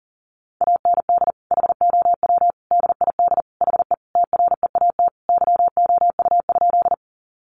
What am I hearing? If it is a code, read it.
Morse code, heard as a keyed tone, about 40 wpm, 700 Hz. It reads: AND HOW DID HE TREAT YOU?
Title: Sentences from Top 1300 Words 40wpm